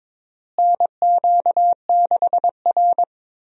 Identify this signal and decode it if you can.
Morse code, heard as a keyed tone, about 22 wpm, 700 Hz. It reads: NQ6R
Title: USA Call Signs - 4 Characters - Repeated 2x - 22wpm